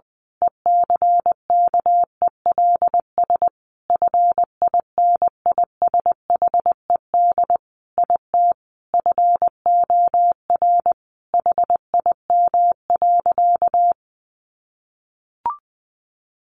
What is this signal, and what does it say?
Morse code, heard as a keyed tone, about 20 wpm, 700 Hz. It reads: ECKELS FINISHED IT FOR HIM.
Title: A Sound Of Thunder 20wpm